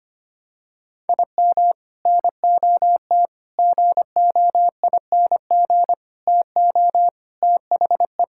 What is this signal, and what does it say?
Morse code, heard as a keyed tone, about 25 wpm, 700 Hz. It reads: IM NOT GOING TO THE
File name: practice / 25wpm / Sentences with 3-Word Phrases - 25wpm